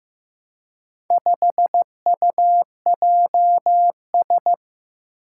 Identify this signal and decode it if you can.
Morse code, heard as a keyed tone, about 15 wpm, 700 Hz. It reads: 5UJS
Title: Four Challenging Letter-Number Combinations 15wpm